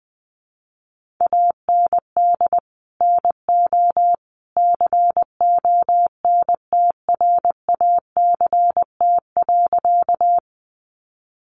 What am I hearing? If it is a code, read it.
Morse code, heard as a keyed tone, about 20 wpm, 700 Hz. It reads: AND NO CONTRACT.
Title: Sentences from Top 1400 Words 20wpm